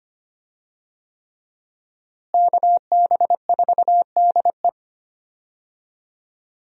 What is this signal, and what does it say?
Morse code, heard as a keyed tone, about 25 wpm, 700 Hz. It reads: KB4DE
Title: USA Call Signs - 5 Characters - Repeated 3x - 25wpm